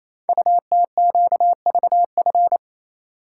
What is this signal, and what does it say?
Morse code, heard as a keyed tone, about 29 wpm, 700 Hz. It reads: UTQVF
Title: Code Groups - 5 Letters - 28wpm